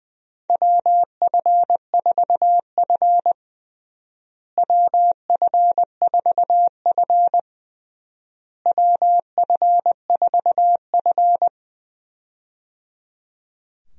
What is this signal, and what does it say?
Morse code, heard as a keyed tone, about 21 wpm, 700 Hz. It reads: WF4F WF4F WF4F
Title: USA Call Signs - 4 Characters - Repeated 3x - 20wpm